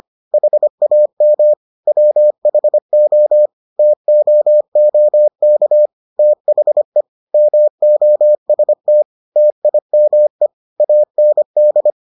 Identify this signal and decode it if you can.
Morse code, heard as a keyed tone, about 25 wpm, 600 Hz. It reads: HAM WHO TOOK THE MOST TIME AND